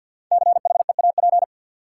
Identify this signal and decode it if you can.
Morse code, heard as a keyed tone, about 52 wpm, 700 Hz. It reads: CHEAP
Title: Five Letter Words - 50wpm